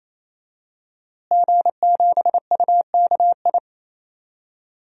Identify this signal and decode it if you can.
Morse code, heard as a keyed tone, about 27 wpm, 700 Hz. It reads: G7UKI
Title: DXCC-100 Call Signs - 5 Characters - Repeated 3x - 28wpm